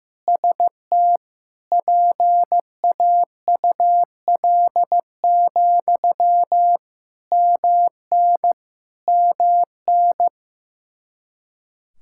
Morse code, heard as a keyed tone, about 15 wpm, 700 Hz. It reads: ST PAUL, MN MN
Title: Practice QSO - Exchange 1 - Beginning - 15wpm